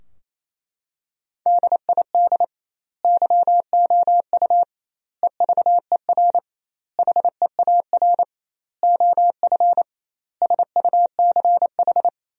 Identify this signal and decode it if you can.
Morse code, heard as a keyed tone, about 28 wpm, 700 Hz. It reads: DID YOU EVER HEAR OF SUCH
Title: 2x Word-Spacing - Sentences from Top 400 Words - 28wpm